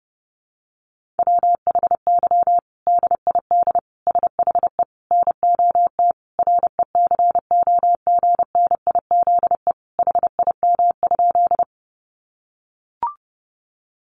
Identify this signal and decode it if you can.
Morse code, heard as a keyed tone, about 30 wpm, 700 Hz. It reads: WHY DID SHE NOT RECOGNIZE HIM?
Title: Sentences from Top 1400 Words 30wpm